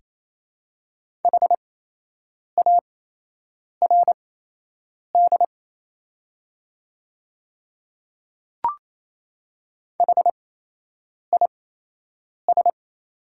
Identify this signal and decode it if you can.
Morse code, heard as a keyed tone, about 27 wpm, spaced as for 8 wpm, 690 Hz. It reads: HARD HIS
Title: 8x Character Spacing - Top 300 Words - 28wpm